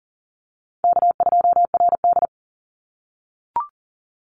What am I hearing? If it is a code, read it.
Morse code, heard as a keyed tone, about 38 wpm, 700 Hz. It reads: K2RD
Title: Call Signs - 40wpm